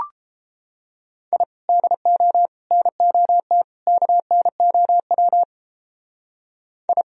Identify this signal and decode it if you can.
Morse code, heard as a keyed tone, about 31 wpm, 700 Hz. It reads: I DO NOT KNOW I